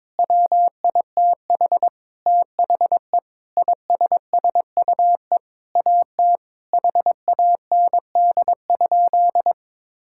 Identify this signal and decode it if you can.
Morse code, heard as a keyed tone, about 22 wpm, 700 Hz. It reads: WITH THE ISSUE AT HAND?
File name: Sentences with 9-Word Phrases - 22wpm